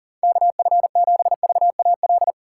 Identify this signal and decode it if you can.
Morse code, heard as a keyed tone, about 40 wpm, 700 Hz. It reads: KF7VAL
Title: USA Call Signs - 6 Characters - 2x Character Spacing - 40wpm